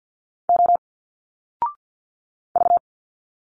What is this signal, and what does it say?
Morse code, heard as a keyed tone, about 50 wpm, 700 Hz. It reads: O V
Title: Single Letters 50wpm